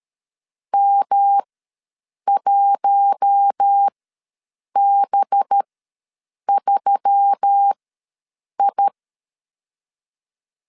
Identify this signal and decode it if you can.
Morse code, heard as a keyed tone, about 13 wpm, spaced as for 7 wpm, 800 Hz. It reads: M1B3I
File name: LEZIONI / 6 PKX0 / 6 PKX ws 12wpm 3106